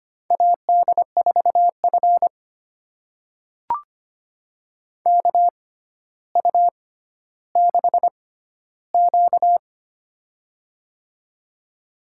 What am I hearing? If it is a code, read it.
Morse code, heard as a keyed tone, about 25 wpm, spaced as for 13 wpm, 700 Hz. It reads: AD4F KU6Q
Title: USA Call Signs - 4 Characters - 6x Character Spacing - 25wpm